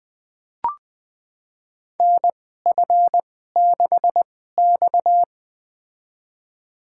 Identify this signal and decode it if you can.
Morse code, heard as a keyed tone, about 20 wpm, spaced as for 15 wpm, 700 Hz. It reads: NF6X
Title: USA Call Signs - 2x Character Spacing - 20wpm